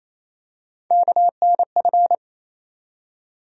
Morse code, heard as a keyed tone, about 28 wpm, 700 Hz. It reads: KNF
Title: USA Call Sign Postfixes - 3 Letters 28wpm